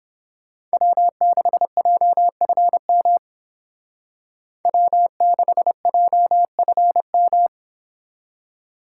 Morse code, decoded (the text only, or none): W6JFM W6JFM